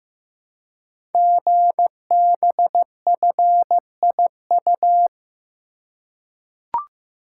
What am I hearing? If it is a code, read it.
Morse code, heard as a keyed tone, about 15 wpm, 700 Hz. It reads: GBFIU
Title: Code Groups - 5 Letters - 15wpm